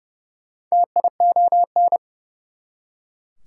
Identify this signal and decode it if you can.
Morse code, heard as a keyed tone, about 40 wpm, 700 Hz. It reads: TION